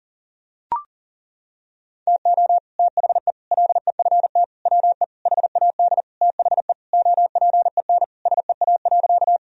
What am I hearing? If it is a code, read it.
Morse code, heard as a keyed tone, about 40 wpm, 700 Hz. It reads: TO THE LEFT WE HAD THE OPEN SEA.
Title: Sentences from Top 1000 Words - Review Part 1 - 40wpm